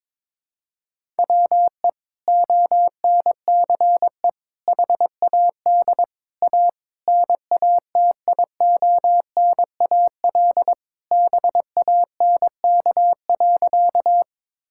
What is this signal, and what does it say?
Morse code, heard as a keyed tone, about 22 wpm, 700 Hz. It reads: WE ONCE HAD A NATIONAL BANK.
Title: Sentences from Top 1000 Words - Review Part 6 - 22wpm